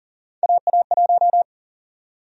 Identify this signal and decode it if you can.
Morse code, heard as a keyed tone, about 40 wpm, 700 Hz. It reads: AA1
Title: USA Call Sign Prefixes - Repeated 2x - 40wpm